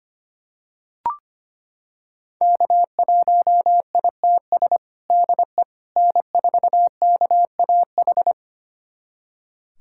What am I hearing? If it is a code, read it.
Morse code, heard as a keyed tone, about 25 wpm, 700 Hz. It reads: K1ITS DE N4KAH
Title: USA Callsign DE Callsign - 5 Characters - 25wpm